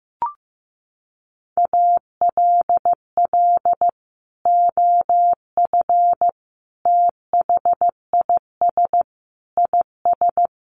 Morse code, heard as a keyed tone, about 15 wpm, 700 Hz. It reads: ALL OF THIS IS
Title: Sentences from Top 1100 Words 15wpm